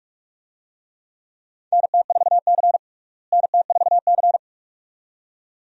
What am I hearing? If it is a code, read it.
Morse code, heard as a keyed tone, about 42 wpm, 700 Hz. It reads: NT4C NT4C
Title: USA Call Signs - 4 Characters - Repeated 2x - 45wpm